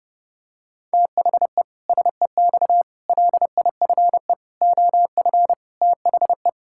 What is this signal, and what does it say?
Morse code, heard as a keyed tone, about 30 wpm, 700 Hz. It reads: THE SEX LIFE OF THE
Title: Pangram Challenge v2 - 30wpm